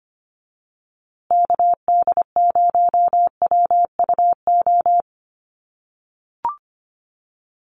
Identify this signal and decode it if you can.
Morse code, heard as a keyed tone, about 25 wpm, 700 Hz. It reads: KD0WUO
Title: USA Call Signs Repeated 3x 25wpm